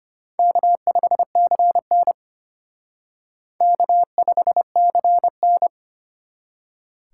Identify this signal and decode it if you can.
Morse code, heard as a keyed tone, about 32 wpm, 700 Hz. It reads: K5CN K5CN